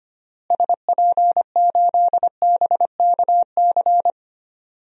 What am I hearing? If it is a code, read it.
Morse code, heard as a keyed tone, about 25 wpm, 700 Hz. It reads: SP8BKC
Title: DXCC-100 Call Signs - 6 Characters - Repeated 2x - 25wpm